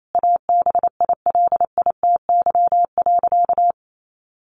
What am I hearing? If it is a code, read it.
Morse code, heard as a keyed tone, about 28 wpm, 700 Hz. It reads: ABILITY.
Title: Sentences from Top 1200 Words 28wpm